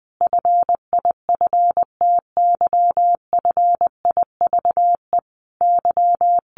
Morse code, heard as a keyed tone, about 20 wpm, 700 Hz. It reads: FIFTY FIVE Y
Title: A Sound Of Thunder 20wpm